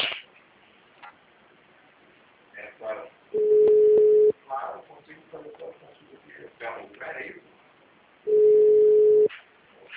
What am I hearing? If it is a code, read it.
Ringback tone (Europe, 425 Hz), with speech.
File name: out- -2021- - - .12569